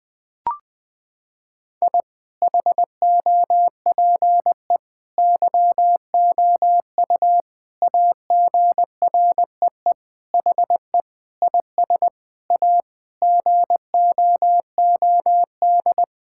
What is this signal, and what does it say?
Morse code, heard as a keyed tone, about 20 wpm, 700 Hz. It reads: I HOPE YOU AGREE HE IS A GOOD
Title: Sentences with 8-Word Phrases - 20wpm